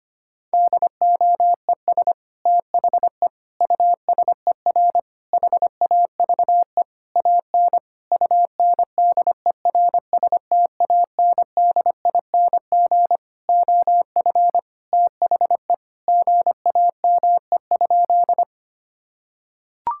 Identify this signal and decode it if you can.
Morse code, heard as a keyed tone, about 25 wpm, 700 Hz. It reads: DOES THE USER HAVE AN UNDERSTANDING OF THE GAME?
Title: Sentences from Top 1000 Words - Review Part 5 - 25wpm